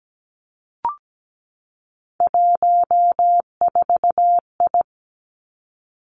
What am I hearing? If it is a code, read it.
Morse code, heard as a keyed tone, about 17 wpm, 700 Hz. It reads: 14I
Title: Three Challenging Letter-Number Combinations 17wpm